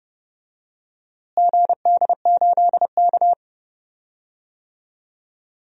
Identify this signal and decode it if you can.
Morse code, heard as a keyed tone, about 31 wpm, 700 Hz. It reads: GD8K
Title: DXCC-100 Call Signs - 4 Characters - Repeated 3x - 30wpm